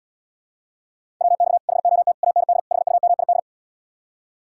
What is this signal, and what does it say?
Morse code, heard as a keyed tone, about 79 wpm, 700 Hz. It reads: WX HOT AND HUMID